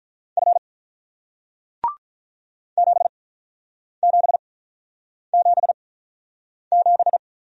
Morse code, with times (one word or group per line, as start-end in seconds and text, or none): 0.4-0.6 F
2.8-7.2 7777